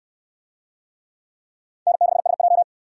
Morse code, heard as a keyed tone, about 59 wpm, 700 Hz. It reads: N6SJ